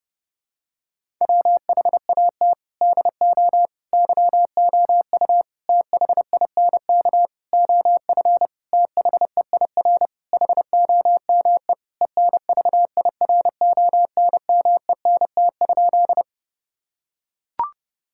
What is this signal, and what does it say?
Morse code, heard as a keyed tone, about 30 wpm, 700 Hz. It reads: WHAT DO YOU THINK OF THEIR HOME ENVIRONMENT?
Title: Sentences from Top 1000 Words - Review Part 5 - 30wpm